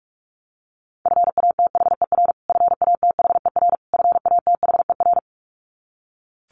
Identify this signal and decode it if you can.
Morse code, heard as a keyed tone, about 44 wpm, 700 Hz. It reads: FATHER FATHER FATHER
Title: Top 200 Words Repeated 45wpm